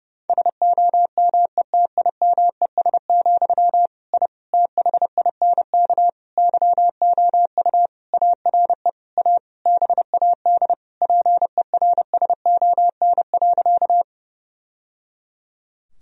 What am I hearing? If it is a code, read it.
Morse code, heard as a keyed tone, about 30 wpm, 700 Hz. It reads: SOMETIMES, I THINK YOU ARE A BAD PERSON.
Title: Sentences from Top 1000 Words - Review Part 3 - 30wpm